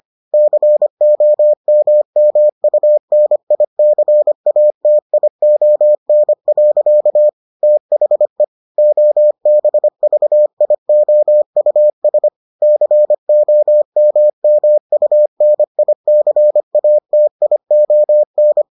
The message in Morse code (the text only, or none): COMMUNICATION. THE OBVIOUS COMMUNICATION